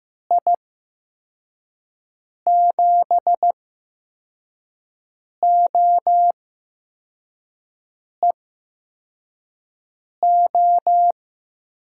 Morse code, heard as a keyed tone, about 15 wpm, spaced as for 4 wpm, 700 Hz. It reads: I7OEO
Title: USA Call Signs - 8x Character Spacing - 15wpm